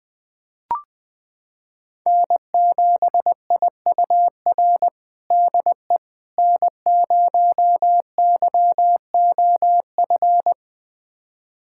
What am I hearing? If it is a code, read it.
Morse code, heard as a keyed tone, about 20 wpm, 700 Hz. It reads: N7IUR DE N0YOF
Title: USA Callsign DE Callsign - 5 Characters - 20wpm